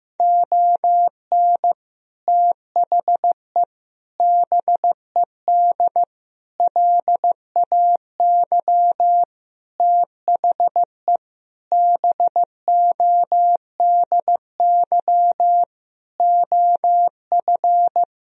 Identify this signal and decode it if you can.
Morse code, heard as a keyed tone, about 15 wpm, 700 Hz. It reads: ON THE BED LAY THE BODY OF
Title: Sentences from Top 1000 Words - Review Part 2 - 15wpm